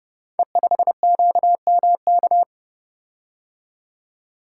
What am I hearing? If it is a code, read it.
Morse code, heard as a keyed tone, about 30 wpm, 700 Hz. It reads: E5QMK
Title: Code Groups - 5 Letter:Numbers - 30wpm